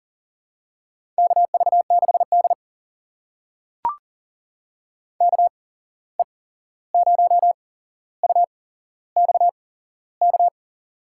Morse code, heard as a keyed tone, about 40 wpm, spaced as for 15 wpm, 700 Hz. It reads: KV6D KE0UXK
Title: USA Call Signs - 8x Character Spacing - 40wpm